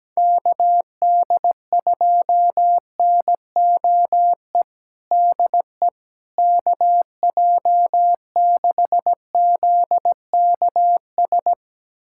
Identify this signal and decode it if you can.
Morse code, heard as a keyed tone, about 17 wpm, 700 Hz. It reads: KD2NOE DE KJ6ZKS
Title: USA Callsign DE Callsign - 6 Characters - 17wpm